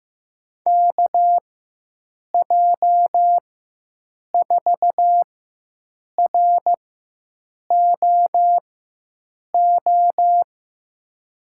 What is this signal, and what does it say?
Morse code, heard as a keyed tone, about 15 wpm, spaced as for 7 wpm, 700 Hz. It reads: KJ4ROO
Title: USA Call Signs - 6 Characters - 4x Character Spacing - 15wpm